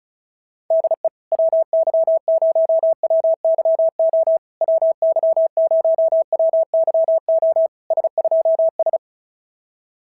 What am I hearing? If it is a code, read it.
Morse code, heard as a keyed tone, about 36 wpm, 650 Hz. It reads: DE WY0WYO WY0WYO S2S
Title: SOTA QSO Practice 35wpm